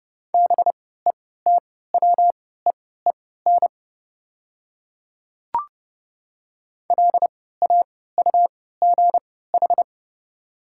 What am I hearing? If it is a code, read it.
Morse code, heard as a keyed tone, about 30 wpm, spaced as for 17 wpm, 700 Hz. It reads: BETWEEN LAUGH
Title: 3x Character Spacing - Top 500 Words - 30wpm